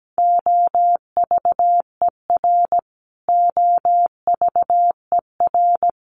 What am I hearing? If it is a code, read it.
Morse code, heard as a keyed tone, about 17 wpm, 700 Hz. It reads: OVER OVER
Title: Top 200 Words Repeated 17wpm